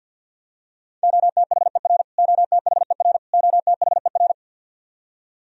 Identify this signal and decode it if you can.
Morse code, heard as a keyed tone, about 52 wpm, 700 Hz. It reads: OTHER OTHER OTHER
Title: Top 100 Words Repeated 50wpm